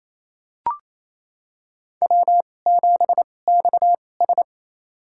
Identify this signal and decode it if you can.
Morse code, heard as a keyed tone, about 28 wpm, spaced as for 20 wpm, 700 Hz. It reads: W7XS
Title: USA Call Signs - 4 Characters - 2x Character Spacing - 28wpm